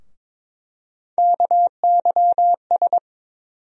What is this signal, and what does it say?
Morse code, heard as a keyed tone, about 21 wpm, 700 Hz. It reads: KYS
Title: USA Call Sign Postfixes - 3 Letters 22wpm